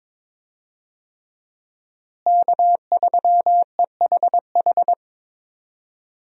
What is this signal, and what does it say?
Morse code, heard as a keyed tone, about 21 wpm, 700 Hz. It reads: K3EHH
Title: USA Call Signs - 5 Characters - Repeated 3x - 22wpm